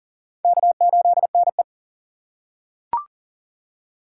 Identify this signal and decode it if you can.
Morse code, heard as a keyed tone, about 40 wpm, 700 Hz. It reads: K8NE
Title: USA Call Signs - 4 Characters - 3x Character Spacing - 40wpm